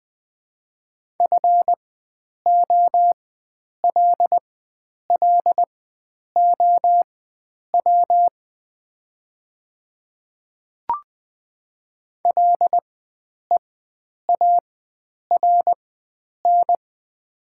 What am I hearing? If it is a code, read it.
Morse code, heard as a keyed tone, about 20 wpm, spaced as for 9 wpm, 700 Hz. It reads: FOLLOW LEARN
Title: 4x Character Spacing - Top 300 Words - 20wpm